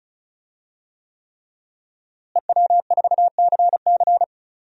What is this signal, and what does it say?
Morse code, heard as a keyed tone, about 36 wpm, 690 Hz. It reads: EW4CC
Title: DXCC-100 Call Signs - 5 Characters - Repeated 3x - 35wpm